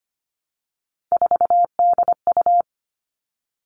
Morse code, heard as a keyed tone, about 25 wpm, 700 Hz. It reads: 4DU
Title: All Cut Numbers - 3-Digit 25wpm